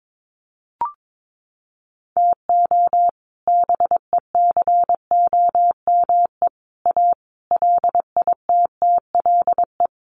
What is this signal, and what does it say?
Morse code, heard as a keyed tone, about 22 wpm, 700 Hz. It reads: TO BECOME A LITTLE
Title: Sentences from Top 1300 Words 22wpm